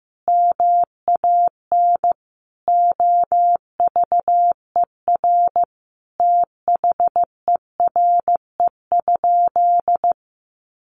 Morse code, heard as a keyed tone, about 15 wpm, 700 Hz. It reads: MAN OVER THERE?
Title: Sentences from Top 200 Words 15wpm